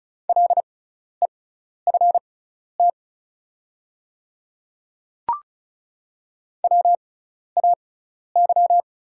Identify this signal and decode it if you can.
Morse code, heard as a keyed tone, about 36 wpm, spaced as for 12 wpm, 700 Hz. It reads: LEFT WAY
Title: 6x Character Spacing - Top 300 Words - 35wpm